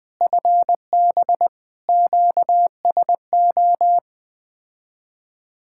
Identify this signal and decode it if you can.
Morse code, heard as a keyed tone, about 20 wpm, 700 Hz. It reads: FB QSO
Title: Speed-Racing Common QSO Phrases - 17wpm